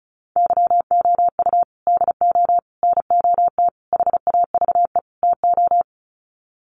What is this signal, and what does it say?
Morse code, heard as a keyed tone, about 34 wpm, 700 Hz. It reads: YOU DO NOT HAVE TO
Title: Sets of 5 Words 35wpm